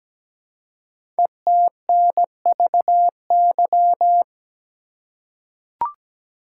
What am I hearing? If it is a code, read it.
Morse code, heard as a keyed tone, about 17 wpm, 700 Hz. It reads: ETNVY
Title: Code Groups - 5 Letters - 17wpm